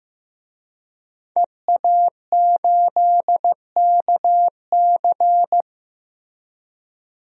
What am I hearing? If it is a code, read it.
Morse code, heard as a keyed tone, about 15 wpm, 700 Hz. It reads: EA8KC
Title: DXCC-100 Call Signs - 5 Characters - Repeated 2x - 15wpm